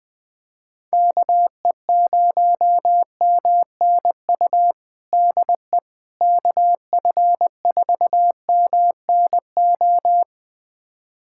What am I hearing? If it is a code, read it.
Morse code, heard as a keyed tone, about 21 wpm, 700 Hz. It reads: KE0MNU DE KF4MNO
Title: USA Callsign DE Callsign - 6 Characters - 20wpm